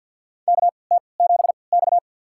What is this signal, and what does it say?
Morse code, heard as a keyed tone, about 51 wpm, spaced as for 28 wpm, 700 Hz. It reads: KT7X
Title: USA Call Signs - 4 Characters - 3x Character Spacing - 50wpm